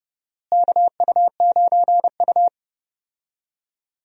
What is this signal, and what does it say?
Morse code, heard as a keyed tone, about 31 wpm, 700 Hz. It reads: KU9U
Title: USA Call Signs - 4 Characters - Repeated 3x - 30wpm